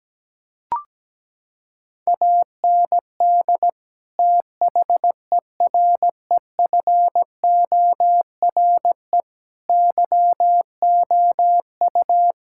Morse code, heard as a keyed tone, about 17 wpm, 700 Hz. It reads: AND THEREFORE YOU
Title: Sentences from Top 1000 Words - Review Part 5 - 17wpm